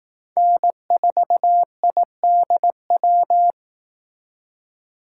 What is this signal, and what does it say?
Morse code, heard as a keyed tone, about 18 wpm, 700 Hz. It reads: N4IDW
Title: Speed Racing - USA 1x3 Call Signs 15wpm